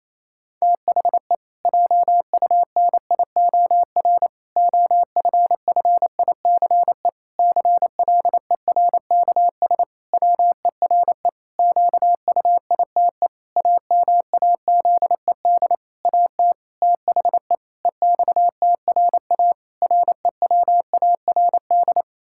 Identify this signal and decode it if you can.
Morse code, heard as a keyed tone, about 28 wpm, 700 Hz. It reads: THE JUNIOR OFFICE CLERKS WERE QUITE AMAZED AT THE EXTRA REWARD
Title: Pangram Challenge v2 - 28wpm